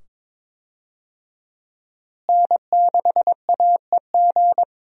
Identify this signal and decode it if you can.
Morse code, heard as a keyed tone, about 22 wpm, 700 Hz. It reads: N6AEG
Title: USA Call Signs - 5 Characters - Repeated 2x - 22wpm